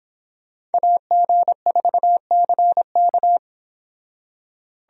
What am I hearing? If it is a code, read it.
Morse code, heard as a keyed tone, about 30 wpm, 700 Hz. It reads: AG4CK